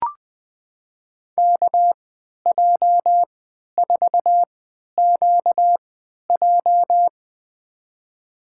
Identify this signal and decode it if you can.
Morse code, heard as a keyed tone, about 20 wpm, spaced as for 11 wpm, 700 Hz. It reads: KJ4QJ
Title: USA Call Signs - 5 Characters - 3x Character Spacing - 20wpm